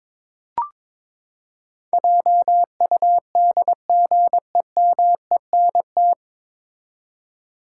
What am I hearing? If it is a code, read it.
Morse code, heard as a keyed tone, about 22 wpm, 700 Hz. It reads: JUDGEMENT
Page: Morse Code Ninja practice 22wpm